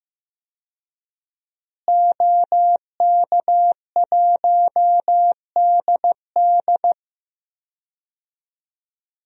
Morse code, heard as a keyed tone, about 15 wpm, 700 Hz. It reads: OK1DD
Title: DXCC-100 Call Signs - 5 Characters - Repeated 2x - 15wpm